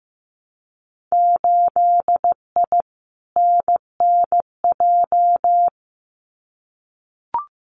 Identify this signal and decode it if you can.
Morse code, heard as a keyed tone, about 15 wpm, 700 Hz. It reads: 8I NNJ
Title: Winter Field Day Exchanges Encore 15wpm